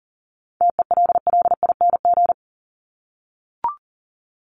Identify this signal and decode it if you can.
Morse code, heard as a keyed tone, about 40 wpm, 700 Hz. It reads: TELLING
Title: ING Suffix - Top 100 Words with Base Words 40wpm